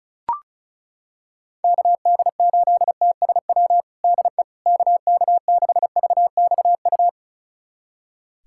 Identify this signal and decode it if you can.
Morse code, heard as a keyed tone, about 36 wpm, 700 Hz. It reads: KD8TSW DE KK6VXU
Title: USA Callsign DE Callsign - 6 Characters - 35wpm